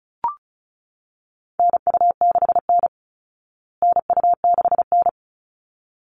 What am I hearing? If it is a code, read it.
Morse code, heard as a keyed tone, about 34 wpm, 700 Hz. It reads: NU6N NU6N
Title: Call Signs Repeated 3x - 35wpm